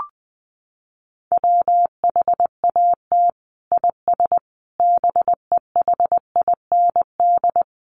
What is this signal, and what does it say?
Morse code, heard as a keyed tone, about 20 wpm, 700 Hz. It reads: WHAT IS BEHIND
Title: Sentences from Top 1100 Words 20wpm